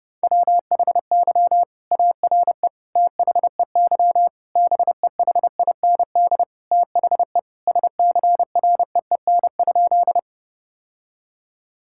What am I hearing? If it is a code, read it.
Morse code, heard as a keyed tone, about 30 wpm, 700 Hz. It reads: WHY ARE THEY BEHIND THE SCREEN?
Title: Sentences from Top 1000 Words - Review Part 6 - 30wpm